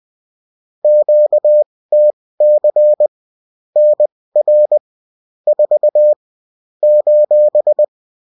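Morse code, heard as a keyed tone, about 20 wpm, spaced as for 16 wpm, 600 Hz. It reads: QTC NR 4 :
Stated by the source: Bulletin in text and CW form.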